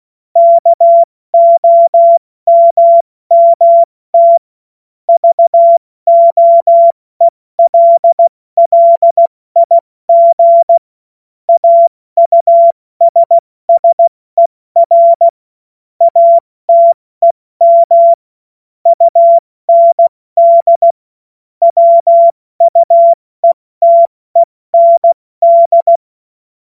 Morse code, heard as a keyed tone, about 16 wpm, 680 Hz. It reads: KOMMT VOELLIG AUSSER ATEM UND WUETEND
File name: Morsekurs / Witze / Witz 06. April 2021, 70 F80